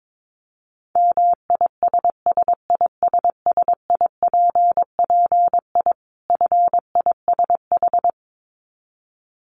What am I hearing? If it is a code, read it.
Morse code, heard as a keyed tone, about 22 wpm, 700 Hz. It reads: MISSISSIPPI FISH
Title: Dit Challenge 22wpm